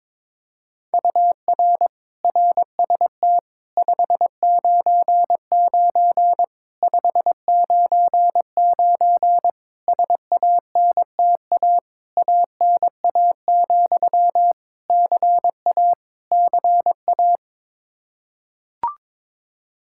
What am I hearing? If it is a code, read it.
Morse code, heard as a keyed tone, about 22 wpm, 700 Hz. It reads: UR RST 599 599 SANTA ANA, CA CA
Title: Practice QSO - Exchange 1 - Beginning - 22wpm